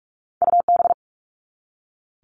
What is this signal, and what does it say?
Morse code, heard as a keyed tone, about 46 wpm, 700 Hz. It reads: UB
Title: All Cut Numbers - 2-Digit 45wpm